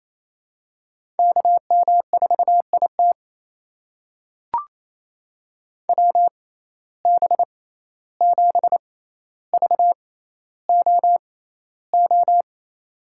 Morse code, heard as a keyed tone, about 28 wpm, spaced as for 14 wpm, 700 Hz. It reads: KM4IT WB7VOO
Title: USA Call Signs - 6x Character Spacing - 28wpm